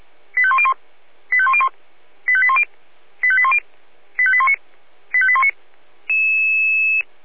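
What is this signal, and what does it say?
Five-tone selective call (ZVEI), heard as quick runs of five tones, tones of about 70 ms. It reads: 85101 85101 85810 85810 85819 85819
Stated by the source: ZVEI five-tone selective call received on SDR.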